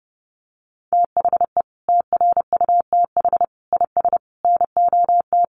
Morse code, heard as a keyed tone, about 30 wpm, 700 Hz. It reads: THE TRUTH IS NOT
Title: Sentences from Top 1500 Words 30wpm